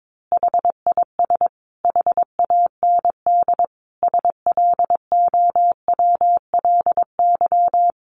Morse code, heard as a keyed tone, about 22 wpm, 700 Hz. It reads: HIS HAND SLOWLY
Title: A Sound Of Thunder 22wpm